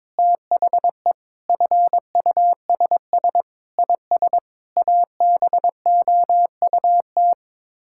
Morse code, heard as a keyed tone, about 22 wpm, 700 Hz. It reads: THE FUSS IS ABOUT
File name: practice / 22wpm / Sets of 8 Words - 22wpm